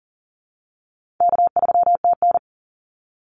Morse code, heard as a keyed tone, about 41 wpm, 700 Hz. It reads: K3TN
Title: Easy Call Signs - 40wpm